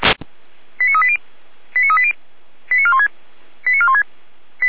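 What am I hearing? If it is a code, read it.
Five-tone selective call (ZVEI), heard as quick runs of five tones, tones of about 70 ms. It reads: 89380 89380 89416 89416
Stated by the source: ZVEI five-tone selective call received on an SDR.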